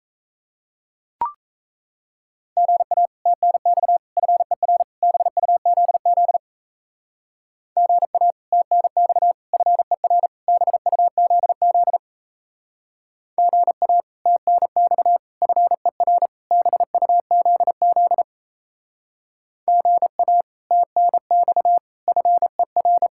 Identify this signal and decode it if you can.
Morse code, heard as a keyed tone, about 35 wpm, 700 Hz. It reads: GA TNX FER BUZZ GA TNX FER BUZZ GA TNX FER BUZZ GA TNX FER
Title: Speed-Racing Common QSO Phrases - 28wpm